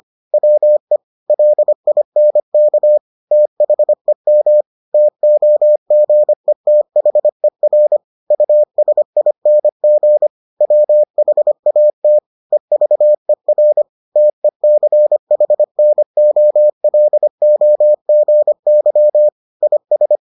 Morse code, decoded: WE LINK THEM TOGETHER USING WHAT EVER TECHNOLOGY IS

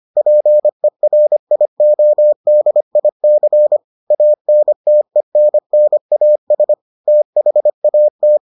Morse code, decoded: PERIODIC ANTENNAS THAT

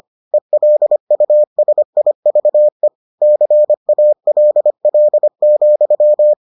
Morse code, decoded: ELUSIVE CALL,

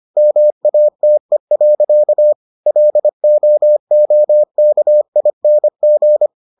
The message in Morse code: MATE. LOOKING

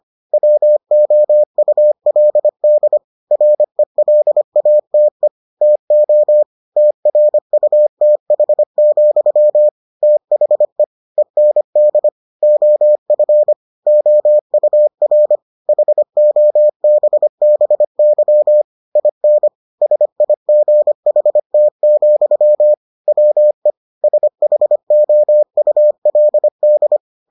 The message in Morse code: WOULD RELATE TO TRUTH, THE END OF OUR HOBBY IN SIGHT, WE SHOULD

